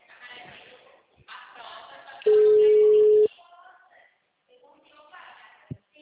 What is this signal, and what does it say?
Ringback tone (Europe, 425 Hz), with speech.